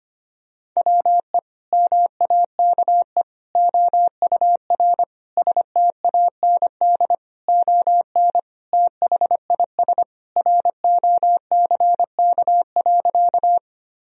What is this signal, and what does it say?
Morse code, heard as a keyed tone, about 25 wpm, 700 Hz. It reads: WE MAKE OUR STAND ON THIS ROCK.
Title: Sentences from Top 1000 Words - Review Part 2 - 25wpm